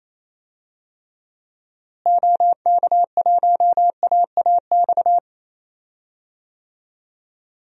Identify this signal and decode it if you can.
Morse code, heard as a keyed tone, about 28 wpm, 700 Hz. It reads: OK1AAX
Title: DXCC-100 Call Signs - 6 Characters - Repeated 2x - 28wpm